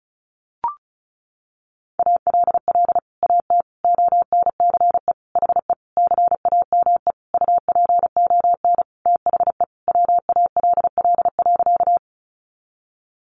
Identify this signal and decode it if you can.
Morse code, heard as a keyed tone, about 34 wpm, 700 Hz. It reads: ALL AT ONCE HE CAME UPON THE WALL.
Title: Sentences from Top 1100 Words 35wpm